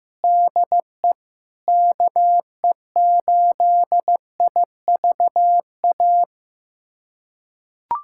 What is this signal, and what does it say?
Morse code, heard as a keyed tone, about 14 wpm, 700 Hz. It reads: DE KE8IVA
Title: USA Callsign DE Callsign - 6 Characters - 15wpm